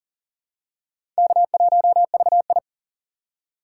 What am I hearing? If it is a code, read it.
Morse code, heard as a keyed tone, about 40 wpm, 700 Hz. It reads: K1VI
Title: USA Call Signs - 4 Characters - 4x Character Spacing - 40wpm